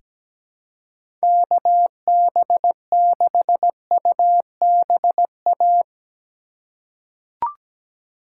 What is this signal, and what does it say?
Morse code, heard as a keyed tone, about 17 wpm, 700 Hz. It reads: KB6UBA
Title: USA Call Signs - 6 Characters - 3x Character Spacing - 17wpm